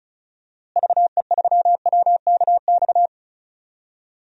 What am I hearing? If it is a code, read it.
Morse code, heard as a keyed tone, about 34 wpm, 700 Hz. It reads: VE3WKX
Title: DXCC-100 Call Signs - 6 Characters - Repeated 3x - 35wpm